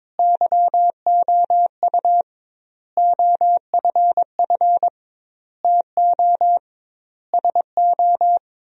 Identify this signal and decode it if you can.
Morse code, heard as a keyed tone, about 21 wpm, 700 Hz. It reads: YOU OFF TO SO
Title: 2x Word-Spacing - Sentences from Top 400 Words - 22wpm